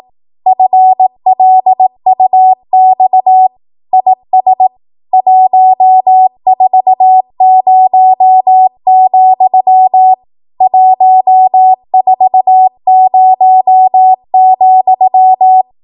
Morse code, heard as a keyed tone, about 18 wpm, 750 Hz. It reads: FLUX IS 140, 140,
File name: Morse / 18WPM